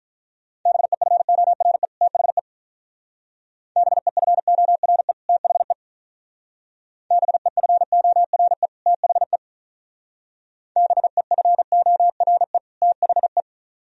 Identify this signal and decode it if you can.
Morse code, heard as a keyed tone, about 44 wpm, 700 Hz. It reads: BEFORE THE BEFORE THE BEFORE THE BEFORE THE